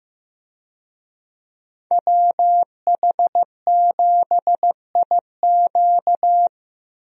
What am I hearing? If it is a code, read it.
Morse code, heard as a keyed tone, about 16 wpm, 700 Hz. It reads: WH7IQ
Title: USA Call Signs - 5 Characters - Repeated 3x - 15wpm